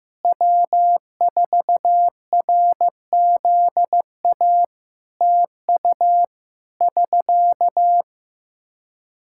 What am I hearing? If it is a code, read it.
Morse code, heard as a keyed tone, about 15 wpm, 700 Hz. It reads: W4RZA TU <SK>
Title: Practice QSO - Exchange 4 - Ending - 15wpm